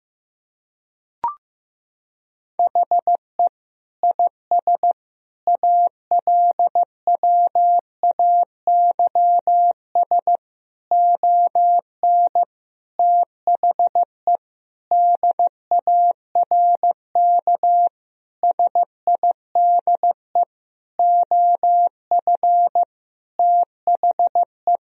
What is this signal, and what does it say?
Morse code, heard as a keyed tone, about 15 wpm, 700 Hz. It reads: HE IS ALWAYS ON THE DARK SIDE OF THE
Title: Sentences from Top 1000 Words - Review Part 3 - 15wpm